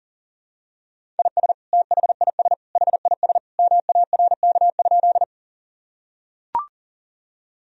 Morse code, heard as a keyed tone, about 40 wpm, 700 Hz. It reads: IS THIS HIS MARK?
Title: Sentences from Top 1000 Words - Review Part 1 - 40wpm